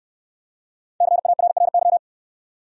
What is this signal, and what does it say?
Morse code, heard as a keyed tone, about 60 wpm, 700 Hz. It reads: CARRY